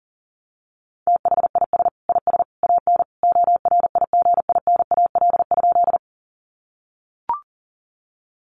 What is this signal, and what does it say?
Morse code, heard as a keyed tone, about 40 wpm, 700 Hz. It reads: THIS IS AN ORIGINAL?
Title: Sentences from Top 900 Words 40wpm